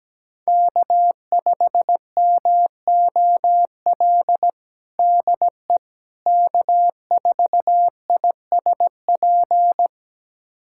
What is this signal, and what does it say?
Morse code, heard as a keyed tone, about 17 wpm, 700 Hz. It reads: K5MOL DE K4ISP
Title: USA Callsign DE Callsign - 5 Characters - 17wpm